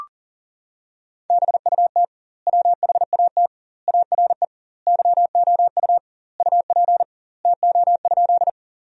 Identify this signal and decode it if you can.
Morse code, heard as a keyed tone, about 40 wpm, 700 Hz. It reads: BUT WHAT ARE YOU UP TO?
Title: 2x Word-Spacing - Sentences from Top 100 Words - 40wpm